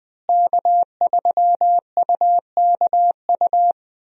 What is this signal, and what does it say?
Morse code, heard as a keyed tone, about 20 wpm, 700 Hz. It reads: K3UKU
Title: USA Call Signs - 5 Characters - Repeated 3x - 20wpm